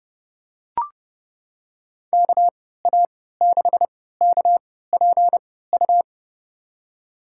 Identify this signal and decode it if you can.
Morse code, heard as a keyed tone, about 31 wpm, spaced as for 17 wpm, 700 Hz. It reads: KA6KPU
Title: USA Call Signs - 6 Characters - 3x Character Spacing - 30wpm